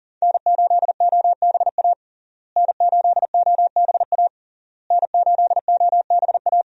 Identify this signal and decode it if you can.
Morse code, heard as a keyed tone, about 40 wpm, 700 Hz. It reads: N8OBA N8OBA N8OBA
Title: USA Call Signs - 5 Characters - Repeated 3x - 40wpm